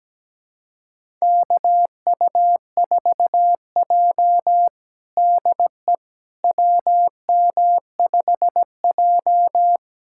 Morse code, heard as a keyed tone, about 17 wpm, 700 Hz. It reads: KU4J DE WM5J
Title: USA Callsign DE Callsign - 4 Characters - 17wpm